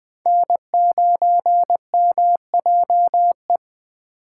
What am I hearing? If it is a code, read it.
Morse code, heard as a keyed tone, about 21 wpm, 700 Hz. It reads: N9MJE
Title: Speed Racing - USA 1x3 Call Signs 17wpm